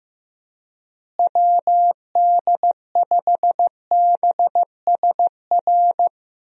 Morse code, heard as a keyed tone, about 15 wpm, 700 Hz. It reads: WD5BSR
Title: USA Call Signs - 6 Characters - Repeated 3x - 15wpm